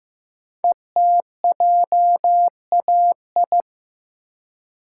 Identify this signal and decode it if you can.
Morse code, heard as a keyed tone, about 15 wpm, 700 Hz. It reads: ETJAI
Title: Code Groups - 5 Letters - 15wpm